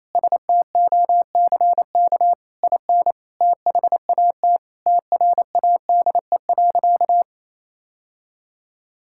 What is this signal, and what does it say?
Morse code, heard as a keyed tone, about 28 wpm, 700 Hz. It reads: STOCK IN THAT TRADE.
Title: Sentences from Top 1000 Words - Review Part 5 - 28wpm